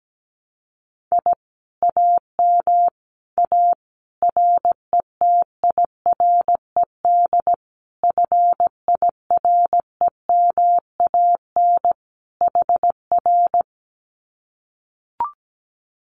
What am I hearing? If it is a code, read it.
Morse code, heard as a keyed tone, about 17 wpm, 700 Hz. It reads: I AM A RETIRED FIREMAN HR
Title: Common QSO Phrases Encore 17wpm